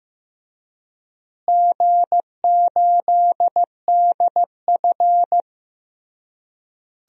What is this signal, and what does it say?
Morse code, heard as a keyed tone, about 16 wpm, 700 Hz. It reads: G8DF
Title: DXCC-100 Call Signs - 4 Characters - Repeated 3x - 15wpm